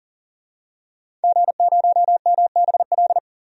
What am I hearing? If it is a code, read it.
Morse code, heard as a keyed tone, about 40 wpm, 700 Hz. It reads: G0MBL
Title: DXCC-100 Call Signs - 5 Characters - 40wpm